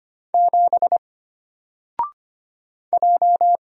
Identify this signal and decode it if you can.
Morse code, heard as a keyed tone, about 25 wpm, 700 Hz. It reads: 7 J
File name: Single-Letter-Number-25wpm